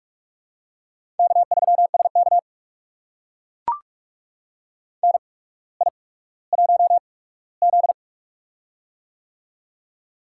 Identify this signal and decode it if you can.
Morse code, heard as a keyed tone, about 44 wpm, spaced as for 19 wpm, 700 Hz. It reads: K3SK NI1Z
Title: USA Call Signs - 4 Characters - 8x Character Spacing - 45wpm